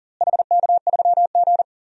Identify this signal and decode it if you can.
Morse code, heard as a keyed tone, about 40 wpm, 700 Hz. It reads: HK3G
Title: DXCC-100 Call Signs - 4 Characters - Repeated 3x - 40wpm